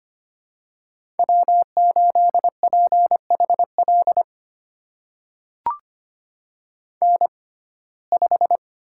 Morse code, heard as a keyed tone, about 25 wpm, spaced as for 18 wpm, 700 Hz. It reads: W8PHL N5
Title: USA Call Signs - 5 Characters - 6x Character Spacing - 25wpm